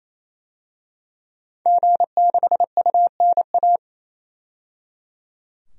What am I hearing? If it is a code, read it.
Morse code, heard as a keyed tone, about 28 wpm, 700 Hz. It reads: G6UNA
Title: DXCC-100 Call Signs - 5 Characters - 28wpm